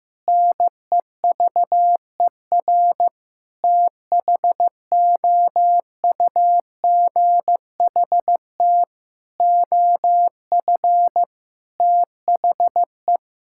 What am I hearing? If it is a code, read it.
Morse code, heard as a keyed tone, about 15 wpm, 700 Hz. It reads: NEVER THOUGHT OF THE
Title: Sentences from Top 1600 Words 15wpm